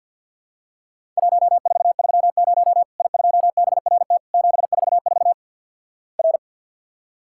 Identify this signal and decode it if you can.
Morse code, heard as a keyed tone, about 50 wpm, 700 Hz. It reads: 1430 I2BRT 744 R
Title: Worked All Europe DX Contest - QTC Exchange - 50wpm